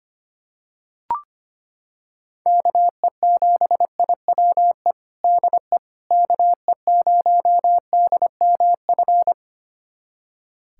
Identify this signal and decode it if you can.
Morse code, heard as a keyed tone, about 25 wpm, 700 Hz. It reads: KE7IWE DE KE0DMF
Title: USA Callsign DE Callsign 25wpm